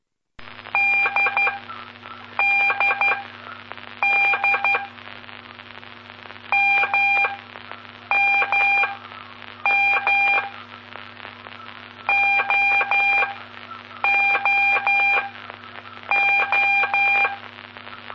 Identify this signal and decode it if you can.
Morse code, heard as a keyed tone, about 12 wpm, spaced as for 7 wpm, 2400 Hz, with noise. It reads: DDD MMM OOO
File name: pp5bar / aulas de telegrafia / Aula-09